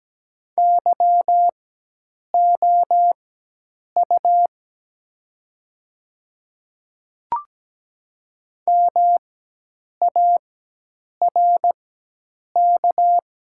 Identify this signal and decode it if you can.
Morse code, heard as a keyed tone, about 17 wpm, spaced as for 8 wpm, 700 Hz. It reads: YOU MARK
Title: 4x Character Spacing - Top 500 Words - 17wpm